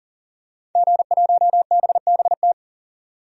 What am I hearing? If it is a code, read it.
Morse code, heard as a keyed tone, about 40 wpm, 700 Hz. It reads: G1BBT
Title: DXCC-100 Call Signs - 5 Characters - 40wpm